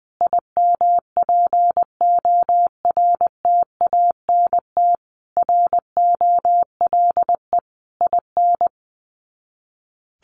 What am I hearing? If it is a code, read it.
Morse code, heard as a keyed tone, about 20 wpm, 700 Hz. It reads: IMPORTANT ROLE IN
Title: Sets of 5 Words Encore2 20wpm